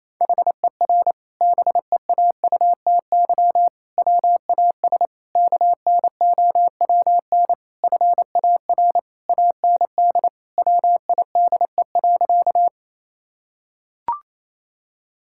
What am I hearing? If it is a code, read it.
Morse code, heard as a keyed tone, about 28 wpm, 700 Hz. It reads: HER BEAUTY WAS KNOWN FAR AND WIDE.
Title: Sentences from Top 1000 Words - Review Part 6 - 28wpm